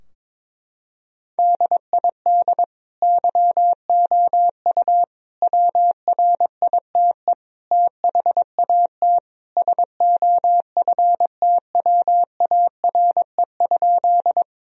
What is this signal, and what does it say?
Morse code, heard as a keyed tone, about 22 wpm, 700 Hz. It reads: DID YOU WRITE THAT SOFTWARE?
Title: Sentences from Top 1000 Words - Review Part 3 - 22wpm